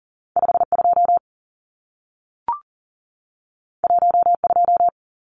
Morse code, heard as a keyed tone, about 40 wpm, 700 Hz. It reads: 52 12
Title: Numbers - 2-Digit 40wpm